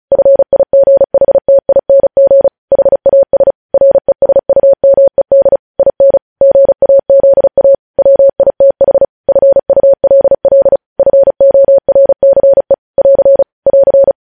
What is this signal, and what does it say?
Morse code, heard as a keyed tone, about 36 wpm, 550 Hz. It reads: FIGHTING HAS RESUMED IN GAZA WITH FULL FORCE <AR> <AR>
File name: podcast / News headlines-1x-35x35